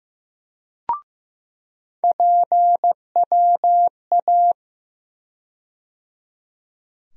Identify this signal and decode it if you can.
Morse code, heard as a keyed tone, about 15 wpm, 700 Hz. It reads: PWA
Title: USA Call Sign Postfixes - 3 Letters 15wpm